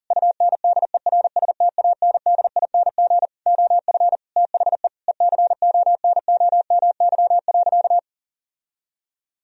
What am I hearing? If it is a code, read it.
Morse code, heard as a keyed tone, about 40 wpm, 700 Hz. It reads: UNDERSTANDING OF THE ECONOMY.
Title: Sentences from Top 1000 Words - Review Part 3 - 40wpm